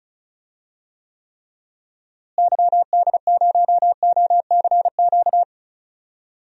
Morse code, heard as a keyed tone, about 36 wpm, 700 Hz. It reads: YD0OCQ
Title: DXCC-100 Call Signs - 6 Characters - Repeated 3x - 35wpm